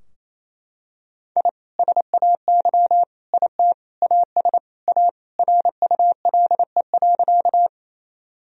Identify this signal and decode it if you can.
Morse code, heard as a keyed tone, about 28 wpm, 700 Hz. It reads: I SAY IT AS A RULE.
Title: Sentences from Top 1000 Words - Review Part 2 - 28wpm